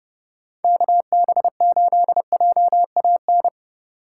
Morse code, heard as a keyed tone, about 28 wpm, 700 Hz. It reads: KB8JAN